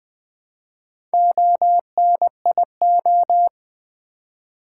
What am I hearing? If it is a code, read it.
Morse code, heard as a keyed tone, about 21 wpm, 700 Hz. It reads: ONIO
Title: Build A Word - 1 Letter At A Time - 5-Letter Words - 20wpm